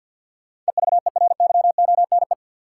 Morse code, heard as a keyed tone, about 53 wpm, 700 Hz. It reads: EVERYONE
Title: Sentences from Top 800 Words - New Word Drill - 50wpm